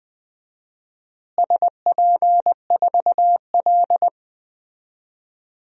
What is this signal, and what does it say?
Morse code, heard as a keyed tone, about 20 wpm, 700 Hz. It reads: SP4L
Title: DXCC-100 Call Signs - 4 Characters - Repeated 3x - 20wpm